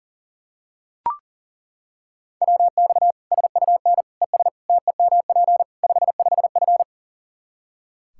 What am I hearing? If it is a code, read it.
Morse code, heard as a keyed tone, about 40 wpm, 700 Hz. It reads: WX SUN ES TEMP 55F
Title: Practice QSO - Exchange 2 - Info - 40wpm